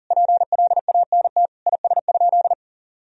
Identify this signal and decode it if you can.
Morse code, heard as a keyed tone, about 37 wpm, 700 Hz. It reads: PLANT IS?
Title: Sentences from Top 1000 Words - Review Part 1 - 40wpm